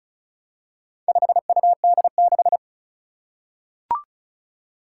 Morse code, heard as a keyed tone, about 36 wpm, 700 Hz. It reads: 5UD6
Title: Four Challenging Letter-Number Combinations 35wpm